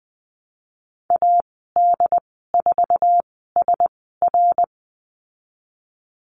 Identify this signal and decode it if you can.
Morse code, heard as a keyed tone, about 20 wpm, spaced as for 15 wpm, 700 Hz. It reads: AD4SR
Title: 088 Lesson 15g - Number 5 - Call Signs 20wpm